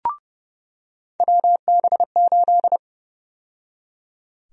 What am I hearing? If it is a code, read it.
Morse code, heard as a keyed tone, about 31 wpm, 700 Hz. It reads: WB8
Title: USA Call Sign Prefixes - 30wpm